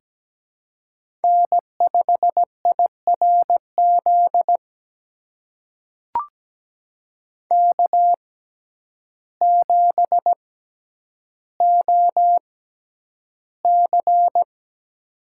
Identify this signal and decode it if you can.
Morse code, heard as a keyed tone, about 17 wpm, spaced as for 9 wpm, 700 Hz. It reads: N5IRZ K7OC
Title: USA Call Signs - 6x Character Spacing - 17wpm